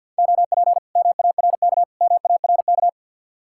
Morse code, heard as a keyed tone, about 66 wpm, spaced as for 58 wpm, 700 Hz. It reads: OP MARK MARK
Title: Speed-Racing Common QSO Phrases - 50wpm